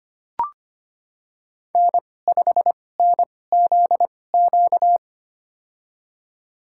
Morse code, heard as a keyed tone, about 25 wpm, spaced as for 18 wpm, 700 Hz. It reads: N5NZQ
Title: USA Call Signs - 5 Characters - 2x Character Spacing - 25wpm